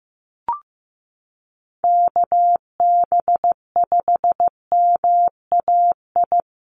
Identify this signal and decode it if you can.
Morse code, heard as a keyed tone, about 15 wpm, 700 Hz. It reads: KB5MAI
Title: USA Call Signs 15wpm